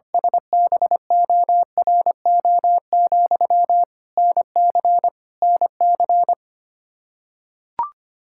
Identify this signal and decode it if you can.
Morse code, heard as a keyed tone, about 25 wpm, 700 Hz. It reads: SBORO, NC NC
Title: Practice QSO - Exchange 1 - Beginning - 25wpm